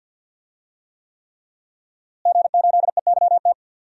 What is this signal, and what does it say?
Morse code, heard as a keyed tone, about 52 wpm, 700 Hz. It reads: G8EYT
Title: DXCC-100 Call Signs - 5 Characters - Repeated 3x - 50wpm